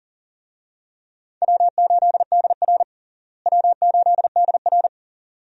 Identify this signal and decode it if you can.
Morse code, heard as a keyed tone, about 40 wpm, 700 Hz. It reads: W8DR W8DR
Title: USA Call Signs - 4 Characters - Repeated 2x - 40wpm